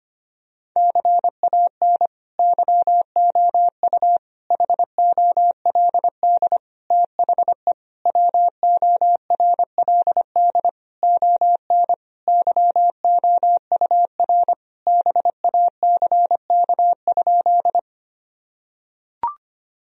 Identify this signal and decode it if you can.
Morse code, heard as a keyed tone, about 25 wpm, 700 Hz. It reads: CAN YOU HOLD THE WORLD ON YOUR BACK?
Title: Sentences from Top 1000 Words - Review Part 2 - 25wpm